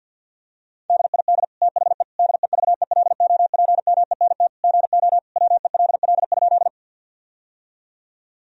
Morse code, heard as a keyed tone, about 50 wpm, 700 Hz. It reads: DID THE DEVELOPMENT GO WELL?
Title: Sentences from Top 1000 Words - Review Part 4 - 50wpm